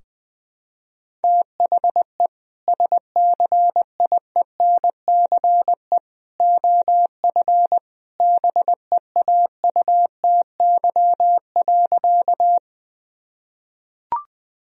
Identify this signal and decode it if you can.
Morse code, heard as a keyed tone, about 20 wpm, 700 Hz. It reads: THE SCIENCE OF BEAUTY.
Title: Sentences from Top 1000 Words - Review Part 2 - 20wpm